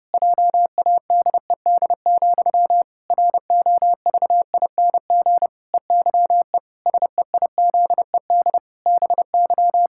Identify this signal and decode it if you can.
Morse code, heard as a keyed tone, about 30 wpm, 700 Hz. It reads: JADED, ROVING EYE SEIZED BY
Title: Pangram Challenge v2 - 30wpm